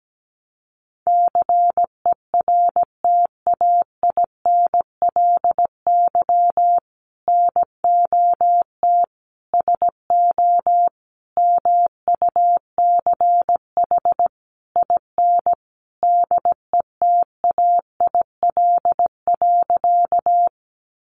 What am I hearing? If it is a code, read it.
Morse code, heard as a keyed tone, about 17 wpm, 700 Hz. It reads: CERTAINLY NOT SO MUCH IN DETAIL.
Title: Sentences from Top 1500 Words 17wpm